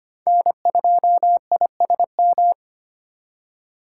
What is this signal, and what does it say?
Morse code, heard as a keyed tone, about 33 wpm, 700 Hz. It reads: N2ISM